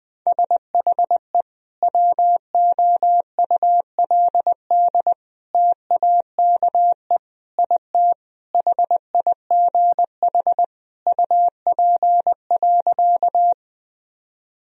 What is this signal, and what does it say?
Morse code, heard as a keyed tone, about 21 wpm, 700 Hz. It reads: SHE WOULD TAKE IT HIGH UP.
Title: Sentences from Top 1000 Words - Review Part 1 - 20wpm